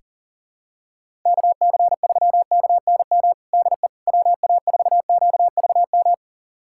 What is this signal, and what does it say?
Morse code, heard as a keyed tone, about 37 wpm, 700 Hz. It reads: KC3KNM DE WA4QVM
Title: USA Callsign DE Callsign - 6 Characters - 40wpm